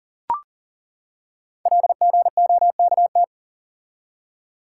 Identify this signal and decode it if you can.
Morse code, heard as a keyed tone, about 40 wpm, 700 Hz. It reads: LGOKT
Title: Code Groups - 5 Letters - 40wpm